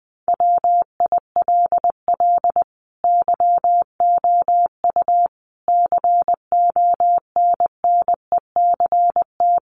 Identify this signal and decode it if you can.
Morse code, heard as a keyed tone, about 20 wpm, 700 Hz. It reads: WILL YOU CONNECT
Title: Sentences from Top 1500 Words 20wpm